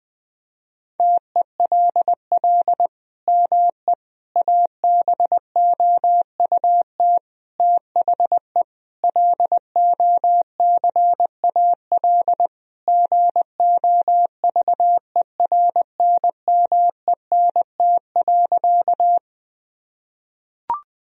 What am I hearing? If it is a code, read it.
Morse code, heard as a keyed tone, about 21 wpm, 700 Hz. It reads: TELL ME ABOUT THE LOCAL GOVERNMENT.
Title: Sentences from Top 1000 Words - Review Part 3 - 20wpm